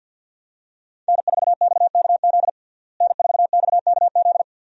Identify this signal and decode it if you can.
Morse code, heard as a keyed tone, about 53 wpm, 700 Hz. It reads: N4XKZ N4XKZ
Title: USA Call Signs - 5 Characters - Repeated 2x - 50wpm